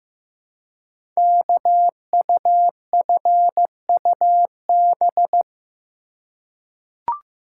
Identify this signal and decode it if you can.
Morse code, heard as a keyed tone, about 15 wpm, 700 Hz. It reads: KUFUB
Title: Code Groups - 5 Letters - 15wpm